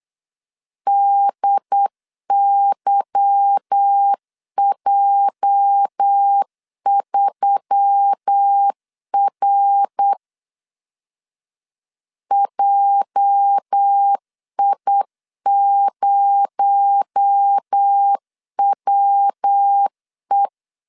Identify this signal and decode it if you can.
Morse code, heard as a keyed tone, about 8 wpm, 800 Hz. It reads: DYJ3R JI0WE
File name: LEZIONI / 8 YJ45 / 8 YJ 8wpm 3313